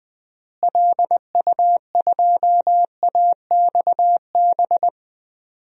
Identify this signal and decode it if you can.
Morse code, heard as a keyed tone, about 20 wpm, 700 Hz. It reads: LU2AXB
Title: DXCC-100 Call Signs - 6 Characters - Repeated 3x - 20wpm